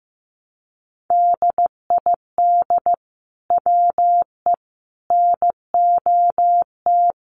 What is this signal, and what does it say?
Morse code, heard as a keyed tone, about 15 wpm, 700 Hz. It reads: DID WE NOT
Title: Sentences from Top 1400 Words 15wpm